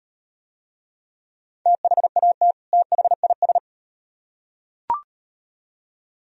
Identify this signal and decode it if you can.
Morse code, heard as a keyed tone, about 38 wpm, 700 Hz. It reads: THAT THIS